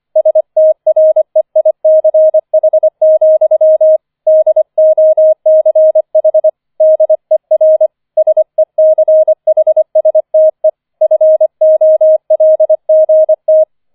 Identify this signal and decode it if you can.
Morse code, heard as a keyed tone, about 25 wpm, 630 Hz. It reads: STREICH, DOCH DER SECHSTE FOLGT
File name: 005 Fünfter Streich 25wpm